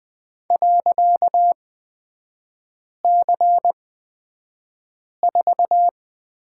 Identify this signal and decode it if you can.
Morse code, heard as a keyed tone, about 20 wpm, spaced as for 5 wpm, 700 Hz. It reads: .C4
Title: 255 Lesson 41e - Period - Three Characters 20wpm